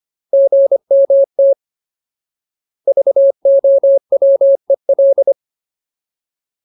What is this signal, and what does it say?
Morse code, heard as a keyed tone, about 25 wpm, 550 Hz. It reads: GMT VOWEL
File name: Ultimate Word List / 25WPM Ultimate